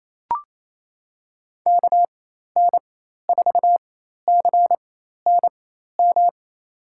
Morse code, heard as a keyed tone, about 27 wpm, spaced as for 13 wpm, 700 Hz. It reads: KN4CNM
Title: USA Call Signs - 6 Characters - 4x Character Spacing - 28wpm